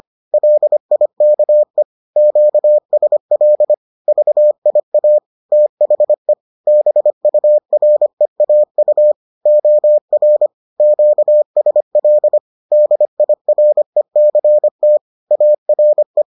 Morse code, heard as a keyed tone, about 25 wpm, 600 Hz. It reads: LIKE QSL VIA THE BUREAU OR QSL DIRECT ARE